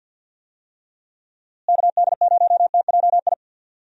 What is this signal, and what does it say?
Morse code, heard as a keyed tone, about 52 wpm, 700 Hz. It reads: KD0TJI
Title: USA Call Signs - 6 Characters - Repeated 3x - 50wpm